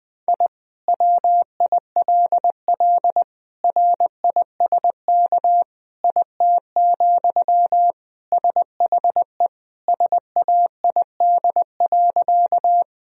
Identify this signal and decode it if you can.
Morse code, heard as a keyed tone, about 20 wpm, 700 Hz. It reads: I WILL RISK IT, SHE SAID.
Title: Sentences from Top 1000 Words - Review Part 4 - 20wpm